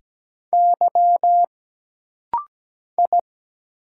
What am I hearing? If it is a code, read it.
Morse code, heard as a keyed tone, about 17 wpm, 700 Hz. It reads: Y I
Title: Single Letters - Rapid-Fire 17wpm